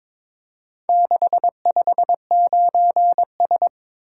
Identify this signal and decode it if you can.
Morse code, heard as a keyed tone, about 22 wpm, 700 Hz. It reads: 659S
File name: practice / 22wpm / Four Letter-Number Combinations - 22wpm